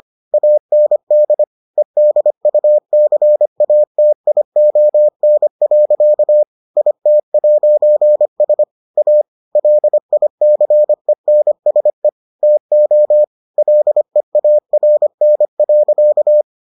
Morse code, heard as a keyed tone, about 25 wpm, 600 Hz. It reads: AND EDUCATION. IT'S A LICENSE TO LEARN.